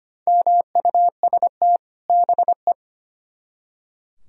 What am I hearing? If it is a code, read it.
Morse code, heard as a keyed tone, about 33 wpm, 700 Hz. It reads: MUST BE